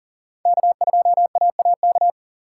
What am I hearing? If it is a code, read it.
Morse code, heard as a keyed tone, about 38 wpm, 700 Hz. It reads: K2AAK
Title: USA Call Signs - 5 Characters - 6x Character Spacing - 40wpm